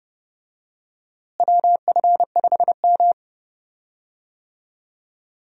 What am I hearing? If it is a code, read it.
Morse code, heard as a keyed tone, about 30 wpm, 700 Hz. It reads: WF5M
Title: USA Call Signs - 4 Characters - Repeated 3x - 30wpm